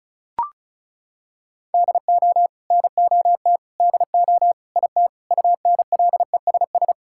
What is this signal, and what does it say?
Morse code, heard as a keyed tone, about 34 wpm, 700 Hz. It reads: DO NOT DO IT UNLESS
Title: Sentences from Top 1000 Words - Review Part 6 - 35wpm